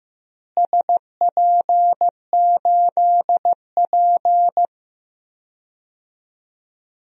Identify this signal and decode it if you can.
Morse code, heard as a keyed tone, about 15 wpm, 700 Hz. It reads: SP8P
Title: DXCC-100 Call Signs - 4 Characters - Repeated 2x - 15wpm